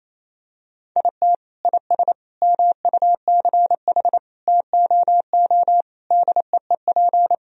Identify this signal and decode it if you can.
Morse code, heard as a keyed tone, about 28 wpm, 700 Hz. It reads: IT IS MUCH TOO DEEP
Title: Sentences from Top 1000 Words - Review Part 3 - 28wpm